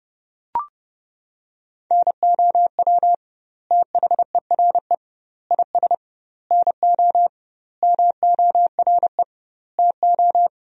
Morse code, heard as a keyed tone, about 30 wpm, 700 Hz. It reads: NOW THERE IS NO MORE TO
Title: 2x Word-Spacing - Sentences from Top 100 Words - 30wpm